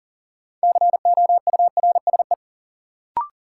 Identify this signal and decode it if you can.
Morse code, heard as a keyed tone, about 40 wpm, 700 Hz. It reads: COURSE
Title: Sentences from Top 500 Words - New Word Drill - 40wpm